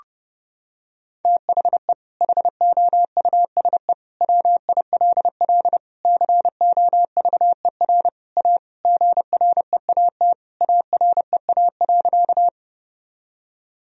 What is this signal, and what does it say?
Morse code, heard as a keyed tone, about 30 wpm, 700 Hz. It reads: THE HOUSE WILL COVER A GREAT AREA.
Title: Sentences from Top 1000 Words - Review Part 2 - 30wpm